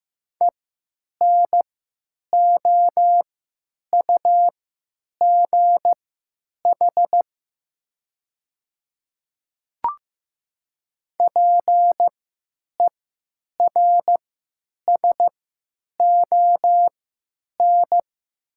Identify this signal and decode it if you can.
Morse code, heard as a keyed tone, about 15 wpm, spaced as for 9 wpm, 700 Hz. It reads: ENOUGH PERSON
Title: 3x Character Spacing - Top 500 Words - 15wpm